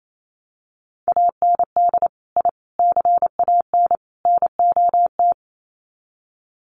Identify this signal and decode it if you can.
Morse code, heard as a keyed tone, about 28 wpm, 700 Hz. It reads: AND I CAN NOT
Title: Sets of 4 Words 28wpm